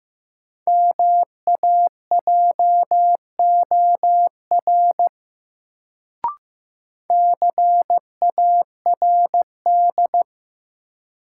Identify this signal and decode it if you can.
Morse code, heard as a keyed tone, about 15 wpm, 700 Hz. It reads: MAJOR CARD
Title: Sentences from Top 600 Words - New Word Drill - 15wpm